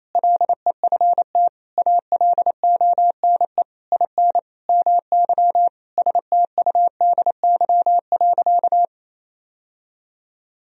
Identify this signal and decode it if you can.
Morse code, heard as a keyed tone, about 28 wpm, 700 Hz. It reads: LEFT ALONE IN MY STUDY.
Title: Sentences from Top 1000 Words - Review Part 1 - 28wpm